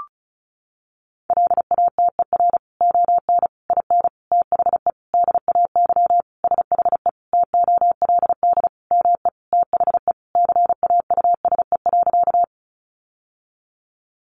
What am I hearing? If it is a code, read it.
Morse code, heard as a keyed tone, about 34 wpm, 700 Hz. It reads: LATER ON IN THE DAY SHE TOLD ME THE CAUSE.
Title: Sentences from Top 700 Words 35wpm